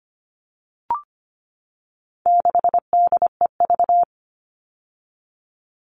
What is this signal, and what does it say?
Morse code, heard as a keyed tone, about 25 wpm, 700 Hz. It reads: 6DEV
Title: All Cut Numbers - 4-Digit 25wpm